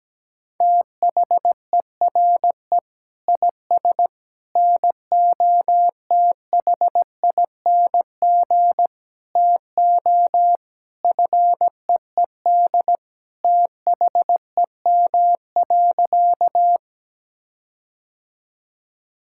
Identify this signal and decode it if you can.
Morse code, heard as a keyed tone, about 17 wpm, 700 Hz. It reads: THERE IS NOTHING TO FEED THEM.
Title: Sentences from Top 1600 Words 17wpm